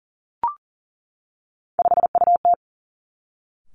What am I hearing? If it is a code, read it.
Morse code, heard as a keyed tone, about 39 wpm, 700 Hz. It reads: 5UT
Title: Three Challenging Letter-Number Combinations 40wpm